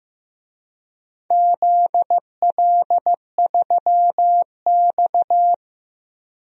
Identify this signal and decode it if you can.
Morse code, heard as a keyed tone, about 15 wpm, 700 Hz. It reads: ZL3X
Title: DXCC-100 Call Signs - 4 Characters - Repeated 2x - 15wpm